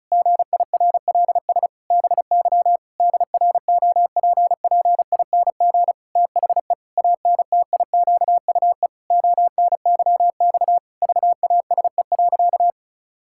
Morse code, heard as a keyed tone, about 36 wpm, 700 Hz. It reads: GIRLS BY DROPPING THE ANTIQUE ONYX VASE.
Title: Pangram Challenge v2 - 35wpm